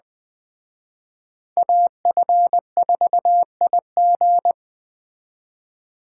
Morse code, heard as a keyed tone, about 21 wpm, 700 Hz. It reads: AF4IG
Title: Speed Racing - USA Call Signs 17wpm